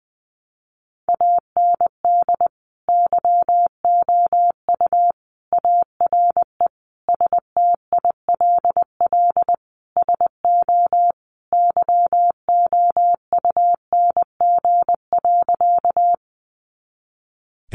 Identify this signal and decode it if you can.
Morse code, heard as a keyed tone, about 20 wpm, 700 Hz. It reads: AND YOU ARE STILL SO YOUNG.
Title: Sentences from Top 400 Words 20wpm